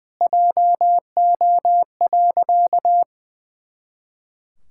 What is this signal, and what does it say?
Morse code, heard as a keyed tone, about 19 wpm, 700 Hz. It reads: JO.
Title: Pangram Challenge v2 - 20wpm